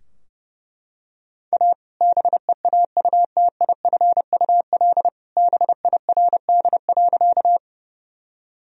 Morse code, heard as a keyed tone, about 30 wpm, 700 Hz. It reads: A BEAUTIFUL BIRD.
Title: Sentences from Top 1000 Words - Review Part 2 - 30wpm